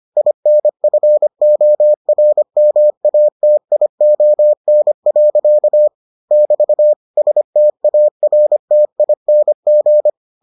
Morse code, heard as a keyed tone, about 25 wpm, 600 Hz. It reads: INFORMATION. <BT> STARTING